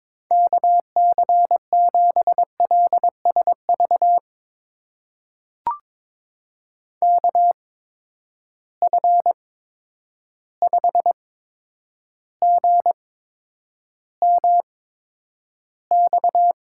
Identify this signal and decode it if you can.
Morse code, heard as a keyed tone, about 22 wpm, spaced as for 9 wpm, 700 Hz. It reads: KC7LSV KF5GMX
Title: USA Call Signs - 6 Characters - 8x Character Spacing - 22wpm